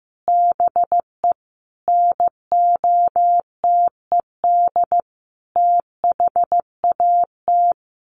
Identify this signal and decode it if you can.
Morse code, heard as a keyed tone, about 14 wpm, 700 Hz. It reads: BE NOTED THAT
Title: Sets of 5 Words 15wpm